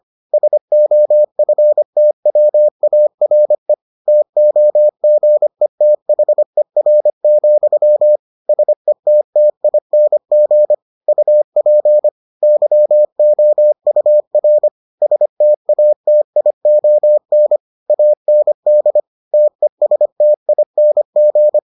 Morse code, heard as a keyed tone, about 25 wpm, 600 Hz. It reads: SOFTWARE TOGETHER, SETTING UP YOUR STATION AND TESTING